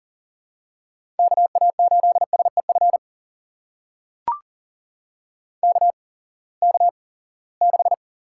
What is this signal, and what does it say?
Morse code, heard as a keyed tone, about 40 wpm, spaced as for 23 wpm, 700 Hz. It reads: KA8SEF KK6
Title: USA Call Signs - 6 Characters - 8x Character Spacing - 40wpm